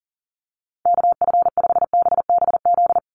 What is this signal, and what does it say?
Morse code, heard as a keyed tone, about 40 wpm, 700 Hz. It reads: KF5BBZ
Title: USA Call Signs 40wpm